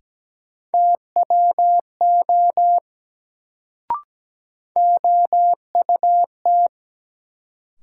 Morse code, heard as a keyed tone, about 17 wpm, 700 Hz. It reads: TWO OUT
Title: Sentences from Top 100 Words - New Word Drill - 17wpm